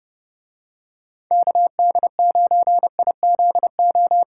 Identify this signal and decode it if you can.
Morse code, heard as a keyed tone, about 31 wpm, 700 Hz. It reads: KD9IZO